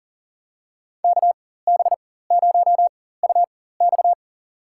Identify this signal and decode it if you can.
Morse code, heard as a keyed tone, about 37 wpm, spaced as for 18 wpm, 700 Hz. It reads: KB0UX
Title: USA Call Signs - 5 Characters - 4x Character Spacing - 40wpm